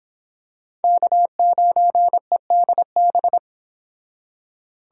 Morse code, heard as a keyed tone, about 26 wpm, 700 Hz. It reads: K9EDB